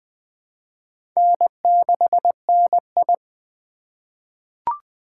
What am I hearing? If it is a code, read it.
Morse code, heard as a keyed tone, about 20 wpm, 700 Hz. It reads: N6NI
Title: USA Call Signs - 4x Character Spacing - 20wpm